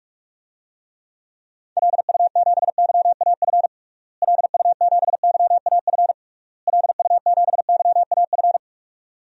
Morse code, heard as a keyed tone, about 46 wpm, 700 Hz. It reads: LU7YAF LU7YAF LU7YAF
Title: DXCC-100 Call Signs - 6 Characters - Repeated 3x - 45wpm